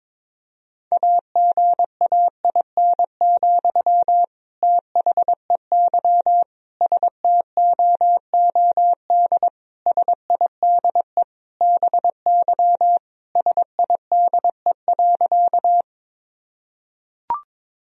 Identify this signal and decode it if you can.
Morse code, heard as a keyed tone, about 22 wpm, 700 Hz. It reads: AGAIN, THEY STOOD SIDE BY SIDE.
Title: Sentences from Top 1000 Words - Review Part 2 - 22wpm